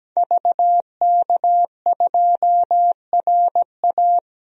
Morse code, heard as a keyed tone, about 17 wpm, 700 Hz. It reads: VK2RA
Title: DXCC-100 Call Signs - 5 Characters - Repeated 3x - 17wpm